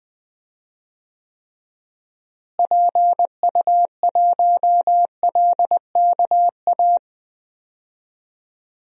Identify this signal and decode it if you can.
Morse code, heard as a keyed tone, about 20 wpm, 700 Hz. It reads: PU1LKA
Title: DXCC-100 Call Signs - 6 Characters - Repeated 3x - 20wpm